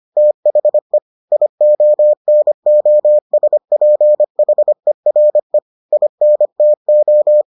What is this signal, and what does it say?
Morse code, heard as a keyed tone, about 25 wpm, 600 Hz. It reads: THE IONOSPHERE INTO